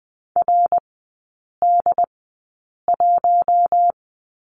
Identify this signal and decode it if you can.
Morse code, heard as a keyed tone, about 20 wpm, spaced as for 8 wpm, 700 Hz. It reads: RD1
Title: 067 Lesson 12e - Letter d - Three Characters 20wpm